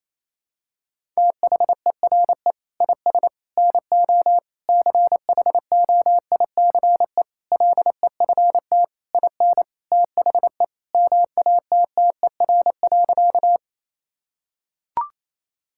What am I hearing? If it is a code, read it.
Morse code, heard as a keyed tone, about 28 wpm, 700 Hz. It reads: THERE IS NO CHOICE LEFT IN THE MATTER.
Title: Sentences from Top 1000 Words - Review Part 6 - 28wpm